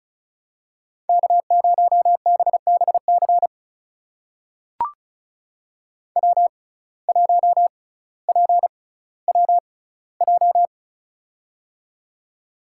Morse code, heard as a keyed tone, about 36 wpm, spaced as for 18 wpm, 700 Hz. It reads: K0BBC W1PWJ
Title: USA Call Signs - 5 Characters - 6x Character Spacing - 35wpm